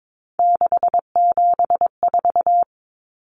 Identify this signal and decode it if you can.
Morse code, heard as a keyed tone, about 22 wpm, 700 Hz. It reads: 674
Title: Common Cut Numbers - 3-Digit 22wpm